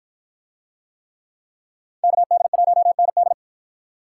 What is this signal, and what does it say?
Morse code, heard as a keyed tone, about 53 wpm, 700 Hz. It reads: KD1ND